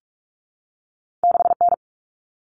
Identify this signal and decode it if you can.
Morse code, heard as a keyed tone, about 46 wpm, 700 Hz. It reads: 6N
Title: All Cut Numbers - 2-Digit 45wpm